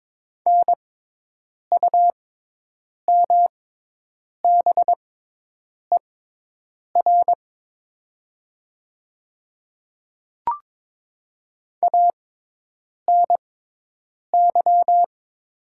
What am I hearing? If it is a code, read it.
Morse code, heard as a keyed tone, about 22 wpm, spaced as for 8 wpm, 700 Hz. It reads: NUMBER ANY
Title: 6x Character Spacing - Top 200 Words - 22wpm